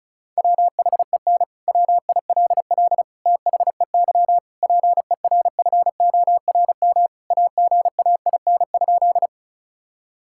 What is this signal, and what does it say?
Morse code, heard as a keyed tone, about 34 wpm, 700 Hz. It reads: WHEN WILL THEY PERFORM AGAIN?
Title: Sentences from Top 1000 Words - Review Part 5 - 35wpm